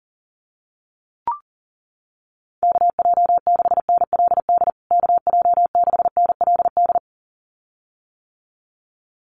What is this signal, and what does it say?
Morse code, heard as a keyed tone, about 40 wpm, 700 Hz. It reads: KJ6NLD KJ6NLD
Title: USA Call Signs Repeated 2x 40wpm